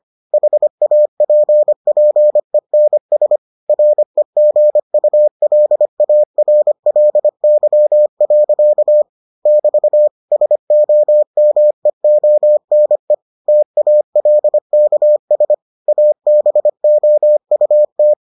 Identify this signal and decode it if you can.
Morse code, heard as a keyed tone, about 25 wpm, 600 Hz. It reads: HAPPENS REGULARLY. <BT> SOMEONE TALKS ABOUT